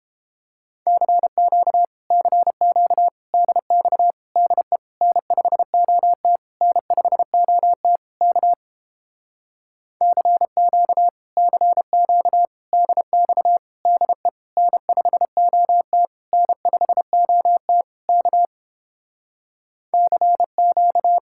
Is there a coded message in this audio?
Yes: Morse code from 0.9 to 21.2 s, heard as a keyed tone, about 29 wpm, 700 Hz. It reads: CQ CQ DX DE N5OT N5OT K CQ CQ DX DE N5OT N5OT K CQ